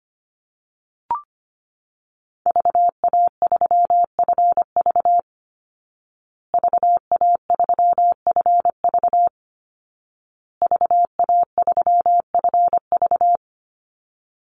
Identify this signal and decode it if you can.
Morse code, heard as a keyed tone, about 25 wpm, 700 Hz. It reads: VA3FV VA3FV VA3FV
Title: Call Signs Repeated 2x - 25wpm